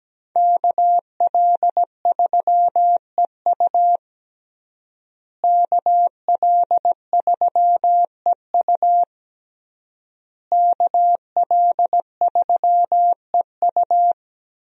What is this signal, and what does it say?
Morse code, heard as a keyed tone, about 17 wpm, 700 Hz. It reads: KL3EU KL3EU KL3EU
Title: USA Call Signs - 5 Characters - Repeated 3x - 17wpm